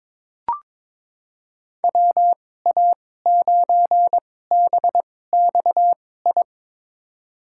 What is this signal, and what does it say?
Morse code, heard as a keyed tone, about 22 wpm, spaced as for 16 wpm, 700 Hz. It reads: WA9BXI
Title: USA Call Signs - 6 Characters - 2x Character Spacing - 22wpm